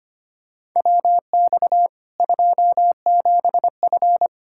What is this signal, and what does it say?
Morse code, heard as a keyed tone, about 25 wpm, 700 Hz. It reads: WX 27F
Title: Practice QSO - Exchange 2 - Info - 25wpm